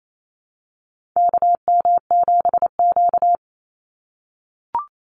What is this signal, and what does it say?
Morse code, heard as a keyed tone, about 28 wpm, 700 Hz. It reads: KM7Q
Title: Call Signs - 28wpm